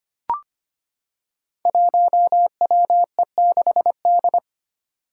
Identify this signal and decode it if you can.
Morse code, heard as a keyed tone, about 25 wpm, 700 Hz. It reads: 1WE6D
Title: Five Challenging Letter-Number Combinations 25wpm